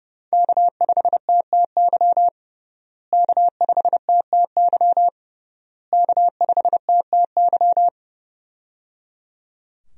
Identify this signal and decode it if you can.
Morse code, heard as a keyed tone, about 31 wpm, 700 Hz. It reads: K5TTY K5TTY K5TTY
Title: USA Call Signs - 5 Characters - Repeated 3x - 30wpm